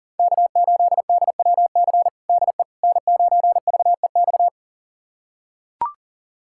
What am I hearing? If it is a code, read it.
Morse code, heard as a keyed tone, about 37 wpm, 700 Hz. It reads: K8DWC DE N9VEX
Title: USA Callsign DE Callsign - 5 Characters - 40wpm